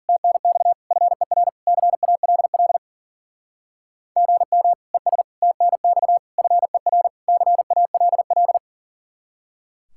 Morse code, heard as a keyed tone, about 50 wpm, 700 Hz. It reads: TNX FER CALL GM ES TNX FER CALL